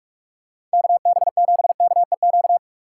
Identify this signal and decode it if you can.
Morse code, heard as a keyed tone, about 49 wpm, 700 Hz. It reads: KB7KEQ